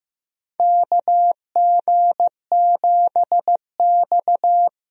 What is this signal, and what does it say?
Morse code, heard as a keyed tone, about 15 wpm, 700 Hz. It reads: KG7X
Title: USA Call Signs - 4 Characters - 15wpm